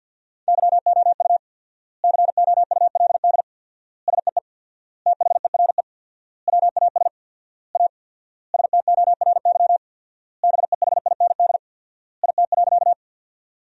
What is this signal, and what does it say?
Morse code, heard as a keyed tone, about 50 wpm, 700 Hz. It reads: YOU COULD SEE THERE WAS A STORY BEHIND IT.
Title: 4x Word-Spacing - Sentences from Top 500 Words - 50wpm